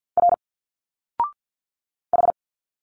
Morse code, heard as a keyed tone, about 50 wpm, 700 Hz. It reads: R H
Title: Single Letters 50wpm